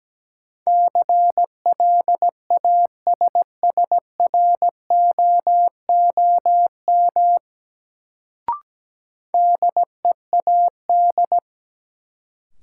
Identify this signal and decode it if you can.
Morse code, heard as a keyed tone, about 17 wpm, 700 Hz. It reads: CLASSROOM DEAD
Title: Sentences from Top 1500 Words - New Word Drill Alt - 17wpm